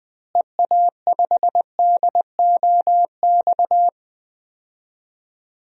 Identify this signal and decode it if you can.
Morse code, heard as a keyed tone, about 20 wpm, 700 Hz. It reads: EA5DOX
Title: DXCC-100 Call Signs - 6 Characters - Repeated 2x - 20wpm